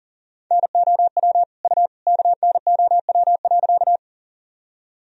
Morse code, heard as a keyed tone, about 38 wpm, 700 Hz. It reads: NOW U KNOW.
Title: Sentences from Top 100 Words and Abbreviations 40wpm